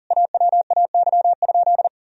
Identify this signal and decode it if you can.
Morse code, heard as a keyed tone, about 40 wpm, 700 Hz. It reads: AWAY?
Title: Sentences from Top 1000 Words - Review Part 1 - 40wpm